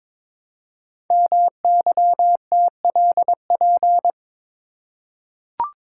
Morse code, heard as a keyed tone, about 23 wpm, 700 Hz. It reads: MYTLP
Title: Code Groups - 5 Letters - 22wpm